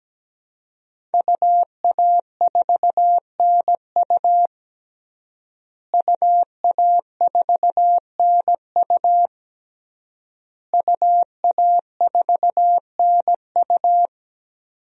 Morse code, heard as a keyed tone, about 17 wpm, 700 Hz. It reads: UA4NU UA4NU UA4NU
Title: DXCC-100 Call Signs - 5 Characters - Repeated 3x - 17wpm